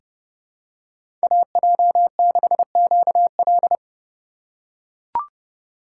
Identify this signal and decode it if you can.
Morse code, heard as a keyed tone, about 32 wpm, 700 Hz. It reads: AJ6QL
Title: K1USN SST QSO Practice - 30wpm